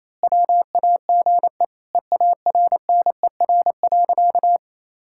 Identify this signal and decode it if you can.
Morse code, heard as a keyed tone, about 28 wpm, 700 Hz. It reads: WAGE EARNER.
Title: Pangram Challenge v2 - 28wpm